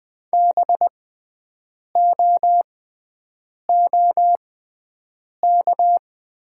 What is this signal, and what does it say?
Morse code, heard as a keyed tone, about 20 wpm, spaced as for 7 wpm, 700 Hz. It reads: BOOK
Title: 6x Character Spacing - Top 500 Words - 20wpm